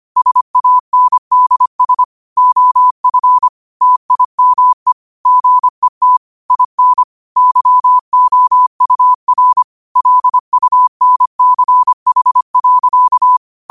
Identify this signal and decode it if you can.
Morse code, heard as a keyed tone, about 25 wpm, 1000 Hz. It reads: SANDS OF TIME GET IN YOUR LUNCH.
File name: qotdfiles / 25wpm